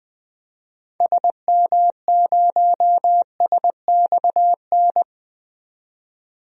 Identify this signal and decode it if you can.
Morse code, heard as a keyed tone, about 20 wpm, 700 Hz. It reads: SM0SXN
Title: DXCC-100 Call Signs - 6 Characters - 20wpm